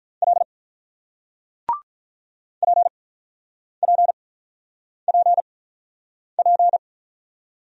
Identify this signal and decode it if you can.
Morse code, heard as a keyed tone, about 50 wpm, spaced as for 9 wpm, 700 Hz. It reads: L PPPP